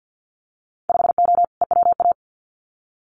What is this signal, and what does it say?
Morse code, heard as a keyed tone, about 50 wpm, 700 Hz. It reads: 5O EWA
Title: Winter Field Day Exchanges Encore 50wpm